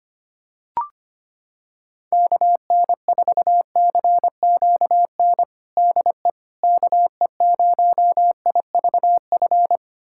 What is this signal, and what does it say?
Morse code, heard as a keyed tone, about 25 wpm, 700 Hz. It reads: KN4CQN DE KE0IVF
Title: USA Callsign DE Callsign - 6 Characters - 25wpm